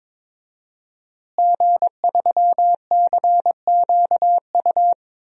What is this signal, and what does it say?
Morse code, heard as a keyed tone, about 22 wpm, 700 Hz. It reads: G3CQU
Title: DXCC-100 Call Signs - 5 Characters - Repeated 2x - 22wpm